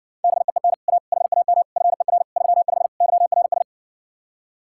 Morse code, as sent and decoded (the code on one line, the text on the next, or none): -... . -. / .- / .... .- -- / ..-. .-. / ...-- ..... / -.-- .-. ...
BEEN A HAM FER 35 YRS